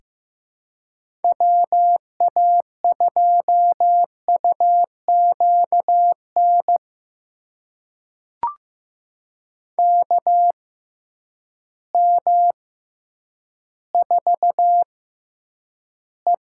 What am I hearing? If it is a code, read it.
Morse code, heard as a keyed tone, about 15 wpm, spaced as for 9 wpm, 700 Hz. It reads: WA2UQN KM4E
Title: USA Call Signs - 6 Characters - 6x Character Spacing - 15wpm